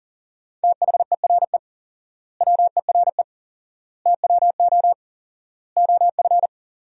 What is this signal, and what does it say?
Morse code, heard as a keyed tone, about 40 wpm, 700 Hz. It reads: THERE WERE TWO OF
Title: 4x Word-Spacing - Sentences from Top 100 Words - 40wpm